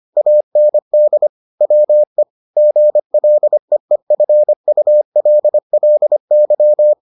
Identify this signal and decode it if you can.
Morse code, heard as a keyed tone, about 25 wpm, 600 Hz. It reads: AND WE GLEEFULLY